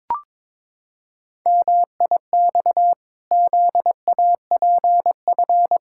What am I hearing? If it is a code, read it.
Morse code, heard as a keyed tone, about 22 wpm, 700 Hz. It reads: MIX ZAPF
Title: Pangram Challenge v2 - 22wpm